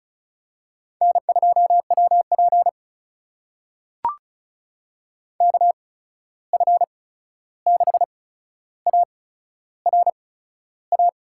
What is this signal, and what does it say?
Morse code, heard as a keyed tone, about 36 wpm, spaced as for 13 wpm, 700 Hz. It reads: N2WP KF6ARA
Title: USA Call Signs - 8x Character Spacing - 35wpm